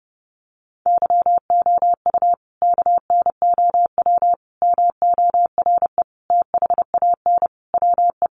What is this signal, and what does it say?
Morse code, heard as a keyed tone, about 30 wpm, 700 Hz. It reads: YOU KNOW MORE THAN WE
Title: Sentences from Top 100 Words 30wpm